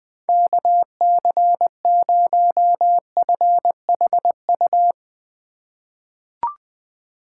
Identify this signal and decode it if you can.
Morse code, heard as a keyed tone, about 20 wpm, 700 Hz. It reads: KC0FHU
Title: USA Call Signs - 6 Characters - 3x Character Spacing - 20wpm